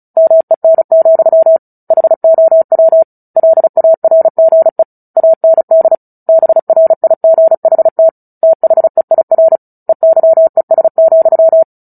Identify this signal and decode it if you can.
Morse code, heard as a keyed tone, about 34 wpm, 650 Hz. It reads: MEN, HOW LARGE AND BRIGHT THEIR EYES,
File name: Book Two / 09-Chapter 9 - 35 WPM